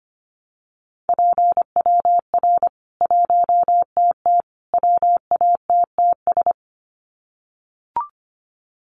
Morse code, heard as a keyed tone, about 25 wpm, 700 Hz. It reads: PWR 1TT WATTS
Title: Common QSO Phrases 25wpm